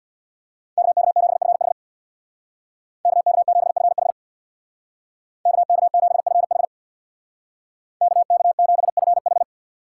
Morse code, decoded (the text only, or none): KK7FH KK7FH KK7FH KK7FH